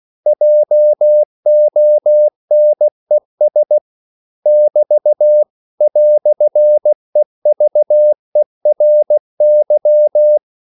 Morse code, heard as a keyed tone, about 16 wpm, 600 Hz. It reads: JONES <BT> "EVERY